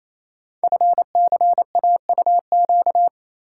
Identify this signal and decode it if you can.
Morse code, heard as a keyed tone, about 29 wpm, 700 Hz. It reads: FCAUQ
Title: Code Groups - 5 Letters - 28wpm